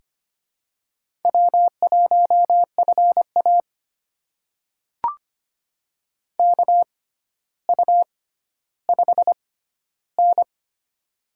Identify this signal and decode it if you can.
Morse code, heard as a keyed tone, about 25 wpm, spaced as for 13 wpm, 700 Hz. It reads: W1FA KU5N
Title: USA Call Signs - 4 Characters - 6x Character Spacing - 25wpm